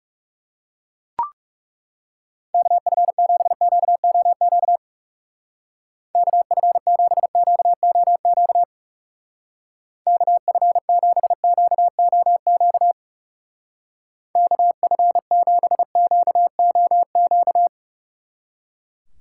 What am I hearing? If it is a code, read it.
Morse code, heard as a keyed tone, about 38 wpm, 700 Hz. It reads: KF7QOQ KF7QOQ KF7QOQ KF7QOQ